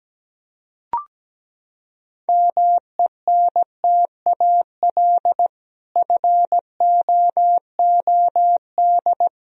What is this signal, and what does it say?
Morse code, heard as a keyed tone, about 17 wpm, 700 Hz. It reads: MENTAL FOOD
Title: Sentences from Top 1000 Words - Review Part 4 - 17wpm